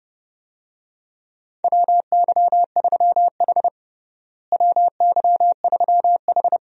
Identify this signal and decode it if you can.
Morse code, heard as a keyed tone, about 30 wpm, 700 Hz. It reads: WY3H WY3H
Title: USA Call Signs - 4 Characters - Repeated 2x - 30wpm